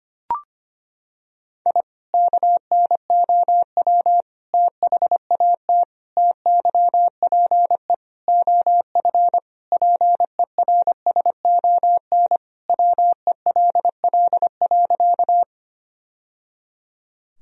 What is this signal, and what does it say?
Morse code, heard as a keyed tone, about 25 wpm, 700 Hz. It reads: I KNOW THAT TYPE OF PERSON WELL.
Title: Sentences from Top 1000 Words - Review Part 3 - 25wpm